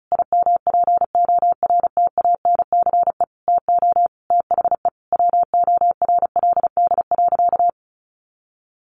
Morse code, heard as a keyed tone, about 34 wpm, 700 Hz. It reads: IMPORTANCE TO THE WORLD.
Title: Sentences from Top 1200 Words 35wpm